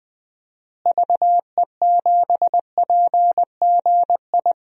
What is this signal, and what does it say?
Morse code, heard as a keyed tone, about 20 wpm, 700 Hz. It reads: VE7PGI
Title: DXCC-100 Call Signs - 6 Characters - Repeated 2x - 20wpm